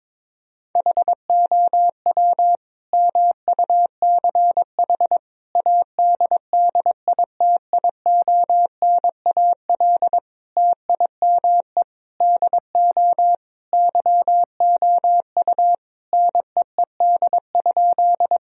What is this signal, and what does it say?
Morse code, heard as a keyed tone, about 21 wpm, 700 Hz. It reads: HOW MUCH ADDITIONAL TIME DO YOU NEED?
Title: Sentences from Top 1000 Words - Review Part 4 - 22wpm